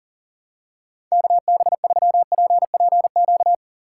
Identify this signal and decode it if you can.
Morse code, heard as a keyed tone, about 37 wpm, 700 Hz. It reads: KB3PPQ
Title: USA Call Signs - 6x Character Spacing - 40wpm